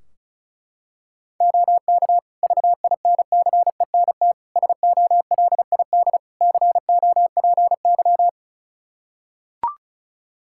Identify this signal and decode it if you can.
Morse code, heard as a keyed tone, about 36 wpm, 700 Hz. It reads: OK VINCENT SOLID COPY
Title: Practice QSO - Exchange 3 - More Info - 35wpm